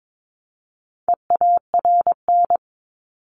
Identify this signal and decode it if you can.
Morse code, heard as a keyed tone, about 22 wpm, 700 Hz. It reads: EARN
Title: Top 1251-1500 Words 22wpm